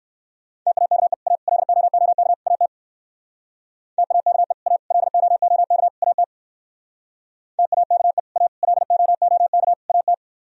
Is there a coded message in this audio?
Yes: Morse code from 0.7 to 10.1 s, heard as a keyed tone, about 57 wpm, 700 Hz. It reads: TAKE A LOOK AT TAKE A LOOK AT TAKE A LOOK AT